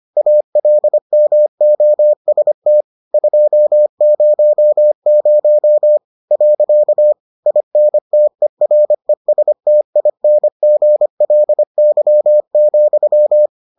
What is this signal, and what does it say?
Morse code, heard as a keyed tone, about 25 wpm, 600 Hz. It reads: ALMOST 200. INTERESTINGLY,